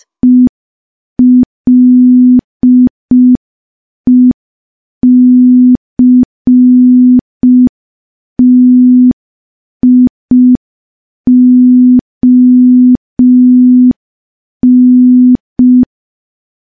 Morse code, heard as a keyed tone, about 5 wpm, 250 Hz. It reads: ELECTION
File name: morse / morse-training-05WPM